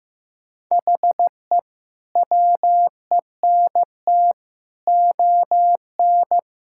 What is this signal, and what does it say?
Morse code, heard as a keyed tone, about 15 wpm, 700 Hz. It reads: HE WENT ON
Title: Sentences from Top 1000 Words - Review Part 6 - 15wpm